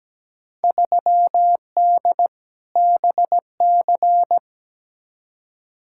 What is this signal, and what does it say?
Morse code, heard as a keyed tone, about 17 wpm, 700 Hz. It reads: 3D BC
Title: Field Day Exchanges Encore - 17wpm